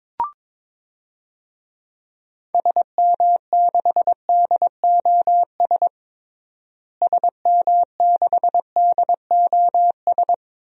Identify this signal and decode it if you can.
Morse code, heard as a keyed tone, about 22 wpm, 700 Hz. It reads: SM6DOS SM6DOS
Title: DXCC-100 Call Signs - 6 Characters - Repeated 2x - 22wpm